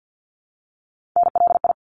Morse code, heard as a keyed tone, about 50 wpm, 700 Hz. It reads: NLI
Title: ARRL Sections 50wpm